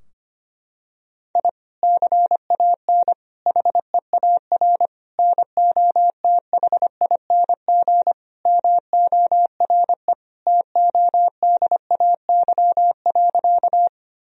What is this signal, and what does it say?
Morse code, heard as a keyed tone, about 25 wpm, 700 Hz. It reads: I CAN HEAR NOTHING MORE TODAY.
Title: Sentences from Top 1000 Words - Review Part 4 - 25wpm